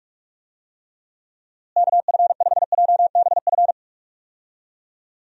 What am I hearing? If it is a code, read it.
Morse code, heard as a keyed tone, about 46 wpm, 700 Hz. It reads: KF5JBF
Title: USA Call Signs - 6 Characters - Repeated 2x - 45wpm